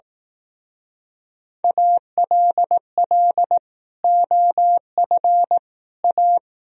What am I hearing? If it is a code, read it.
Morse code, heard as a keyed tone, about 18 wpm, 700 Hz. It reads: ALL OF A
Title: Speed-Racing - Sets of 4 Words 15wpm